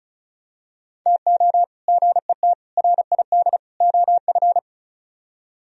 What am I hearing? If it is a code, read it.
Morse code, heard as a keyed tone, about 32 wpm, 700 Hz. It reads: TO GET RID OF
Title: Speed-Racing - Sets of 4 Words 35wpm